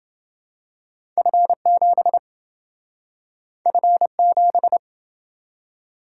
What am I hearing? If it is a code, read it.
Morse code, heard as a keyed tone, about 28 wpm, 700 Hz. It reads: F7 F7